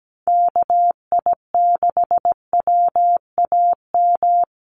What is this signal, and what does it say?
Morse code, heard as a keyed tone, about 17 wpm, 700 Hz. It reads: KI6WAM
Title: USA Call Signs 17wpm